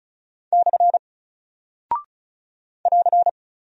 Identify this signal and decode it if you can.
Morse code, heard as a keyed tone, about 36 wpm, 700 Hz. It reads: / <AR>
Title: Single Character (Letter:Number:Punctuation:Prosign) 35wpm